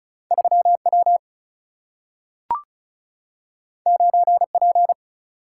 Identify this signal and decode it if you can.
Morse code, heard as a keyed tone, about 34 wpm, 700 Hz. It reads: 3W 9P
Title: Single-Letter Single-Number 35wpm